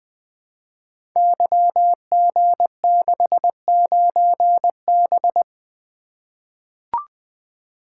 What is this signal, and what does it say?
Morse code, heard as a keyed tone, about 20 wpm, 700 Hz. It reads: YG69B
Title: Code Groups - 5 Letter:Numbers - 20wpm